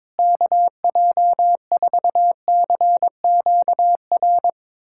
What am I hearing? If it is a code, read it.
Morse code, heard as a keyed tone, about 22 wpm, 700 Hz. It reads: KJ4CQR
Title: USA Callsign DE Callsign 22wpm